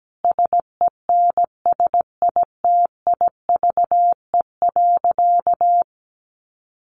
Morse code, heard as a keyed tone, about 17 wpm, 700 Hz. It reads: SENSITIVE.
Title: Sentences from Top 1400 Words 17wpm